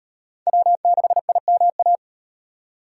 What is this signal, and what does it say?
Morse code, heard as a keyed tone, about 38 wpm, 700 Hz. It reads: W6IMA